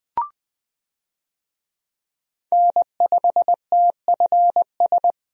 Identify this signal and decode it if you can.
Morse code, heard as a keyed tone, about 20 wpm, 700 Hz. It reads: N5TFS
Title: USA Call Signs - 5 Characters - Repeated 2x - 20wpm